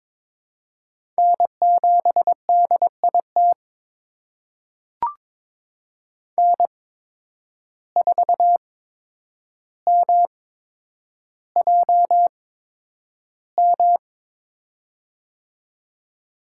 Morse code, heard as a keyed tone, about 22 wpm, spaced as for 9 wpm, 700 Hz. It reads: N7DIT N4MJM
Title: USA Call Signs - 8x Character Spacing - 22wpm